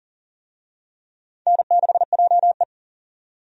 Morse code, heard as a keyed tone, about 40 wpm, 700 Hz. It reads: N6JE
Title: USA Call Signs - 4 Characters - 40wpm